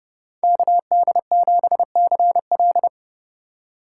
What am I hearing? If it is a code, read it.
Morse code, heard as a keyed tone, about 30 wpm, 700 Hz. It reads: KD7CL
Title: USA Call Signs - 5 Characters - 4x Character Spacing - 30wpm